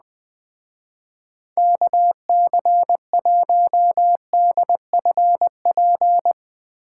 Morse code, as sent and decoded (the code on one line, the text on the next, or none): -.- -.-. .---- -.. ..-. .--.
KC1DFP